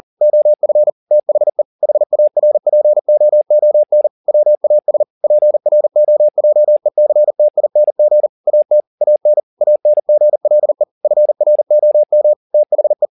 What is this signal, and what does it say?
Morse code, heard as a keyed tone, about 40 wpm, 600 Hz. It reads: OF THE HARPOON WAS PROJECTING AT AN ANGLE FROM THE
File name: the old man and the sea 2 40wpm